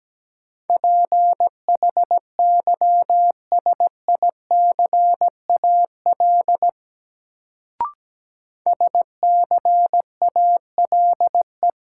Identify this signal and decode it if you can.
Morse code, heard as a keyed tone, about 17 wpm, 700 Hz. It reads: PHYSICAL SCALE
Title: Sentences from Top 1000 Words - Review Part 4 - New Word Drill - 17wpm